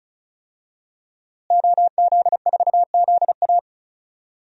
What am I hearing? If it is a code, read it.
Morse code, heard as a keyed tone, about 36 wpm, 700 Hz. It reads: OZ4ZA
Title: DXCC-100 Call Signs - 5 Characters - Repeated 2x - 35wpm